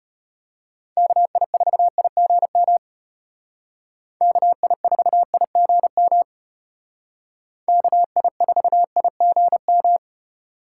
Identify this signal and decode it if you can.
Morse code, heard as a keyed tone, about 33 wpm, 700 Hz. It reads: KI4IGM KI4IGM KI4IGM